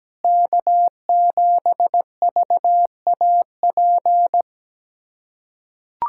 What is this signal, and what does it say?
Morse code, heard as a keyed tone, about 17 wpm, 700 Hz. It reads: K7VAP
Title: K1USN SST QSO Practice - 17wpm